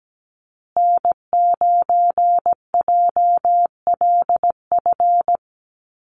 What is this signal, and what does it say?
Morse code, heard as a keyed tone, about 17 wpm, 700 Hz. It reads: N9JLF
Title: USA Call Signs Repeated 2x 17wpm